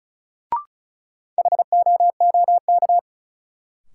Morse code, heard as a keyed tone, about 36 wpm, 700 Hz. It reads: HOOK
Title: Sentences from Top 1600 Words - New Word Drill - 35wpm